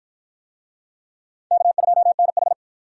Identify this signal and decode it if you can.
Morse code, heard as a keyed tone, about 53 wpm, 700 Hz. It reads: K2NH